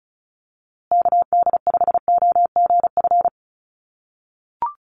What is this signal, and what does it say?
Morse code, heard as a keyed tone, about 34 wpm, 700 Hz. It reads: KD5OGF
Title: USA Call Signs Repeated 2x 35wpm